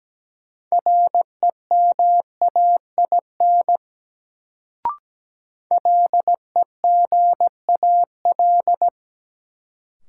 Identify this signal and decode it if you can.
Morse code, heard as a keyed tone, about 17 wpm, 700 Hz. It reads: REMAIN LEGAL
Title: Sentences from Top 1200 Words - New Word Drill Alt - 17wpm